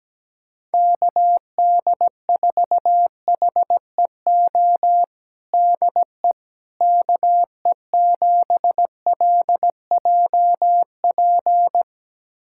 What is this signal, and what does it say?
Morse code, heard as a keyed tone, about 17 wpm, 700 Hz. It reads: KD4HEO DE KE7LJP
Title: USA Callsign DE Callsign 17wpm